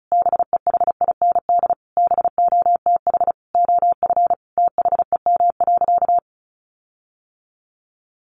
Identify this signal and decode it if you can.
Morse code, heard as a keyed tone, about 34 wpm, 700 Hz. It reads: BEHIND BOTH OF THEM.
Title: Sentences from Top 1400 Words 35wpm